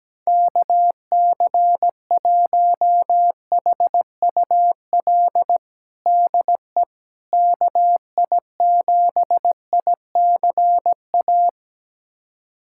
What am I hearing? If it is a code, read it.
Morse code, heard as a keyed tone, about 17 wpm, 700 Hz. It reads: KC1HUL DE KI7ICA
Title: USA Callsign DE Callsign - 6 Characters - 17wpm